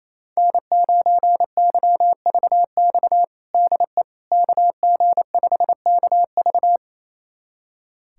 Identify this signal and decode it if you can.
Morse code, heard as a keyed tone, about 28 wpm, 700 Hz. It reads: N9YVX DE KG5KV
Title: USA Callsign DE Callsign - 5 Characters - 28wpm